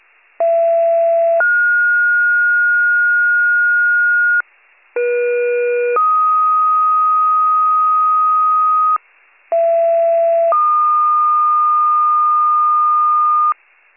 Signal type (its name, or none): two-tone page